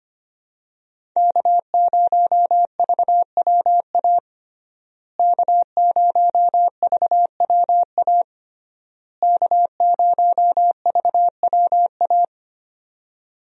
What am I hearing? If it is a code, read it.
Morse code, heard as a keyed tone, about 25 wpm, 700 Hz. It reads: K0VWA K0VWA K0VWA
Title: USA Call Signs - 5 Characters - Repeated 3x - 25wpm